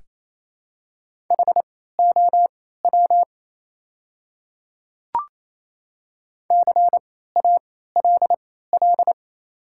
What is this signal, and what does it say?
Morse code, heard as a keyed tone, about 27 wpm, spaced as for 16 wpm, 700 Hz. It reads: HOW CALL
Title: 3x Character Spacing - Top 100 Words - 28wpm